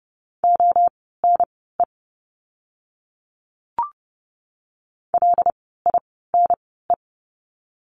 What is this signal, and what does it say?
Morse code, heard as a keyed tone, about 30 wpm, spaced as for 17 wpm, 700 Hz. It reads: ONE LINE
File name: course / 30:10wpm / 115 Lesson 19f - Letter w - Words 30wpm